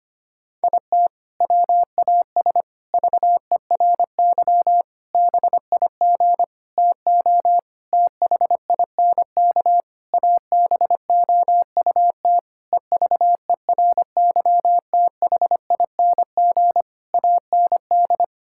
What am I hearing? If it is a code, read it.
Morse code, heard as a keyed tone, about 25 wpm, 700 Hz. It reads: IT WAS VERY BIG TO THINK ABOUT EVERYTHING AND
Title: Sentences from Top 1600 Words 25wpm